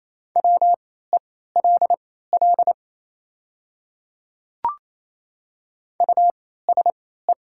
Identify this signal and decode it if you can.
Morse code, heard as a keyed tone, about 28 wpm, spaced as for 16 wpm, 700 Hz. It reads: WELL USE
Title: 3x Character Spacing - Top 200 Words - 28wpm